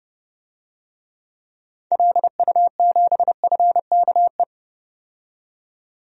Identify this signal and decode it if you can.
Morse code, heard as a keyed tone, about 30 wpm, 700 Hz. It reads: LU7FKE
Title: DXCC-100 Call Signs - 6 Characters - 30wpm